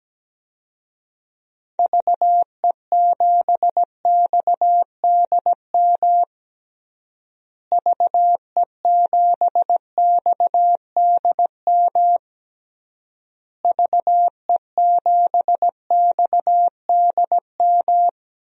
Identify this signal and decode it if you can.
Morse code, heard as a keyed tone, about 17 wpm, 700 Hz. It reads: VE7XDM VE7XDM VE7XDM
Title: DXCC-100 Call Signs - 6 Characters - Repeated 3x - 17wpm